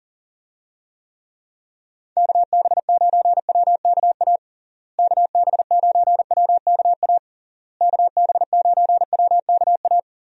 Morse code, heard as a keyed tone, about 40 wpm, 700 Hz. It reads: KB9WKA KB9WKA KB9WKA
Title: USA Call Signs - 6 Characters - Repeated 3x - 40wpm